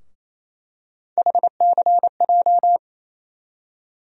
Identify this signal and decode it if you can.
Morse code, heard as a keyed tone, about 28 wpm, 700 Hz. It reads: HCJ
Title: USA Call Sign Postfixes - 3 Letters 28wpm